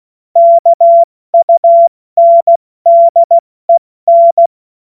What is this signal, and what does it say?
Morse code, heard as a keyed tone, about 16 wpm, 680 Hz. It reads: KUNDEN
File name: Morsekurs / Witze / Witz 10. Mai 2022 70 F80